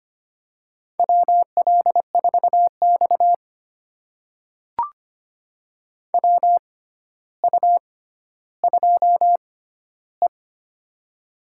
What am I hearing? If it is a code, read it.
Morse code, heard as a keyed tone, about 25 wpm, spaced as for 13 wpm, 700 Hz. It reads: WL4X WU2E
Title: USA Call Signs - 4 Characters - 6x Character Spacing - 25wpm